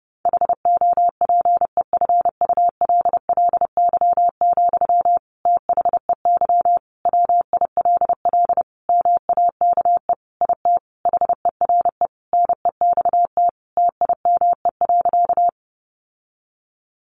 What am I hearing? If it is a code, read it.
Morse code, heard as a keyed tone, about 30 wpm, 700 Hz. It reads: HOPEFULLY, THEY WILL MAKE IT HERE NEXT TIME.
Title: Sentences from Top 1500 Words 30wpm